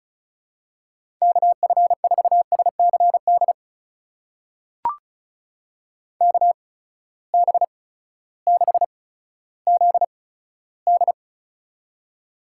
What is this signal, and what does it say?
Morse code, heard as a keyed tone, about 36 wpm, spaced as for 16 wpm, 700 Hz. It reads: KF4SCD KB6ZD
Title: USA Call Signs - 8x Character Spacing - 35wpm